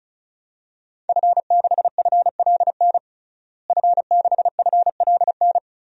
Morse code, heard as a keyed tone, about 36 wpm, 700 Hz. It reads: F6FLN F6FLN
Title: DXCC-100 Call Signs - 5 Characters - Repeated 3x - 35wpm